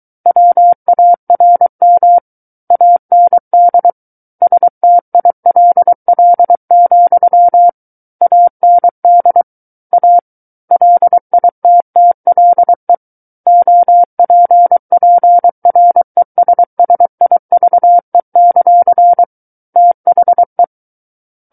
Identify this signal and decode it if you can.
Morse code, heard as a keyed tone, about 23 wpm, 700 Hz. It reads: WARM AND STILL, AND A LITTLE OPPRESSIVE; THE
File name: Book One / 14-Chapter 14 - 23 WPM